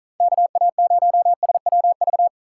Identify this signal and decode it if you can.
Morse code, heard as a keyed tone, about 44 wpm, 700 Hz. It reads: KA0SWV